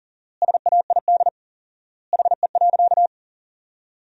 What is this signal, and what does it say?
Morse code, heard as a keyed tone, about 40 wpm, 700 Hz. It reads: SAID HE.
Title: 4x Word-Spacing - Sentences from Top 400 Words - 40wpm